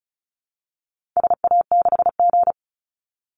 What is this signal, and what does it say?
Morse code, heard as a keyed tone, about 34 wpm, 700 Hz. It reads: SA6G
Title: Call Signs Repeated 3x - 35wpm